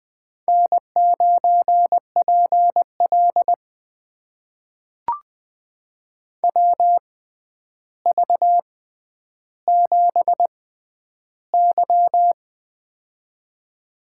Morse code, heard as a keyed tone, about 20 wpm, spaced as for 10 wpm, 700 Hz. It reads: N9PL WV7Y
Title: USA Call Signs - 4 Characters - 6x Character Spacing - 20wpm